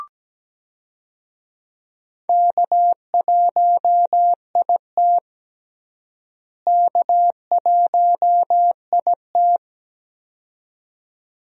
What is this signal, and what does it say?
Morse code, heard as a keyed tone, about 17 wpm, 700 Hz. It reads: K1IT K1IT
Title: USA Call Signs - 4 Characters - Repeated 2x - 17wpm